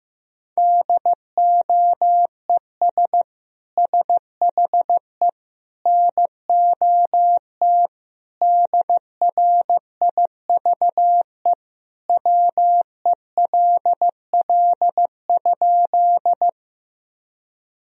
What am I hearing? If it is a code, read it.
Morse code, heard as a keyed tone, about 15 wpm, 700 Hz. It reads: DOES SHE NOT DRIVE WELL?
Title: Sentences from Top 1000 Words - Review Part 2 - 15wpm